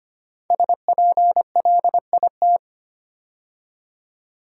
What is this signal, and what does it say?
Morse code, heard as a keyed tone, about 25 wpm, 700 Hz. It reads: SPLIT
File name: 5 Letter Words - First 3 Letters Given - 25wpm